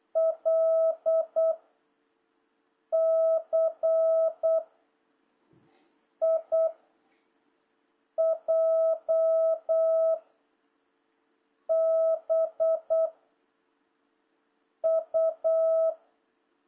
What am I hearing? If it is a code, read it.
Morse code, heard as a keyed tone, about 8 wpm, 660 Hz. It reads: L C I J B U
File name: jh8cbh / 2026cw